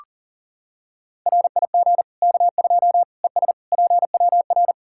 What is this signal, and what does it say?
Morse code, heard as a keyed tone, about 40 wpm, 700 Hz. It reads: RIG K2 ES PWR
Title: Practice QSO - Exchange 2 - Info - 40wpm